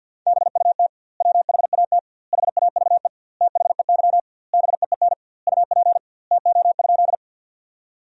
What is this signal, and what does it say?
Morse code, heard as a keyed tone, about 52 wpm, 700 Hz. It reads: BUT WHAT HAVE THEY BEEN UP TO?
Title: 2x Word-Spacing - Sentences from Top 100 Words - 50wpm